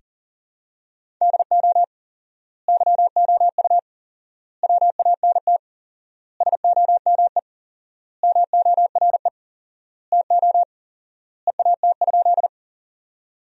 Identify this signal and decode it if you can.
Morse code, heard as a keyed tone, about 40 wpm, 700 Hz. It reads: DO YOU WANT SOME MORE TO EAT?
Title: 4x Word-Spacing - Sentences from Top 300 Words - 40wpm